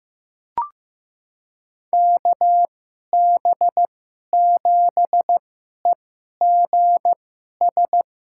Morse code, heard as a keyed tone, about 15 wpm, spaced as for 11 wpm, 700 Hz. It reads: KB7EGS
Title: USA Call Signs - 2x Character Spacing - 15wpm